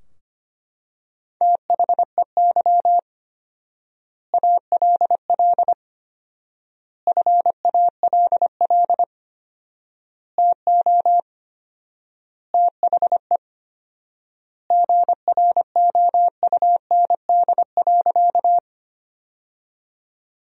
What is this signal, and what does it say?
Morse code, heard as a keyed tone, about 25 wpm, 700 Hz. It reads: THEY ALL FALL TO THE GROUND.
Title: 4x Word-Spacing - Sentences from Top 500 Words - 25wpm